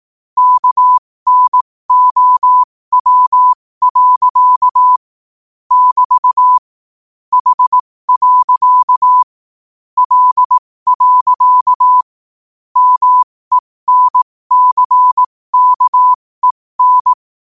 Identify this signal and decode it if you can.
Morse code, heard as a keyed tone, about 18 wpm, spaced as for 15 wpm, 1000 Hz. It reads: KNOW. <BT> H. L. MENCKEN
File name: qotdfiles / 15wpm